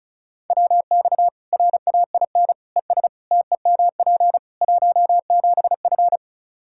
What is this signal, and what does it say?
Morse code, heard as a keyed tone, about 36 wpm, 700 Hz. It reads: WX RAIN ES TEMP 17F
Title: Practice QSO - Exchange 2 - Info - 35wpm